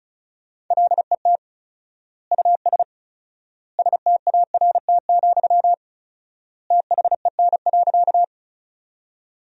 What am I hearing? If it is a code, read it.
Morse code, heard as a keyed tone, about 36 wpm, 700 Hz. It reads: LET US START, THEN.
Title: 4x Word-Spacing - Sentences from Top 300 Words - 35wpm